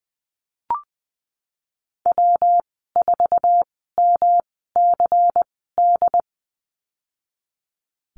Morse code, heard as a keyed tone, about 20 wpm, spaced as for 15 wpm, 700 Hz. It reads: W4MCD
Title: 238 Lesson 38g - Letter q - Call Signs 20wpm